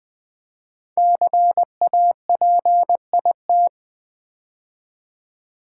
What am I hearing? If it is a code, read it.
Morse code, heard as a keyed tone, about 20 wpm, 700 Hz. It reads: CAPIT
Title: Build A Word - 5 Letters At A Time - 10-Letter Words - 20wpm